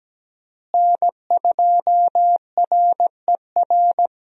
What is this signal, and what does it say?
Morse code, heard as a keyed tone, about 17 wpm, 700 Hz. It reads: N2RER
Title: USA Call Signs - 5 Characters - Repeated 2x - 17wpm